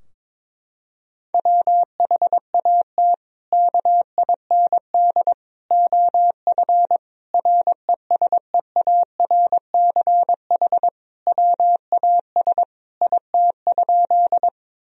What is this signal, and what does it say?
Morse code, heard as a keyed tone, about 22 wpm, 700 Hz. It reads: WHAT KIND OF RESEARCH WAS IT?
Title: Sentences from Top 1000 Words - Review Part 4 - 22wpm